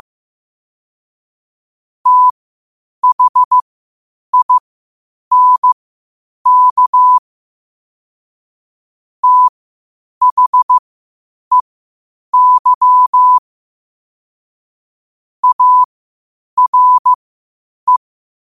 Morse code, heard as a keyed tone, about 15 wpm, spaced as for 8 wpm, 1000 Hz. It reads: THINK THEY ARE
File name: qotdfiles / 8wpm